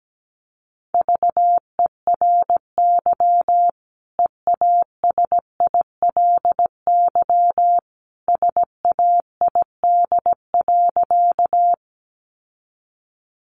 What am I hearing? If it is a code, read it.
Morse code, heard as a keyed tone, about 17 wpm, 700 Hz. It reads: VERY EASILY SAID.
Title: Sentences from Top 900 Words 17wpm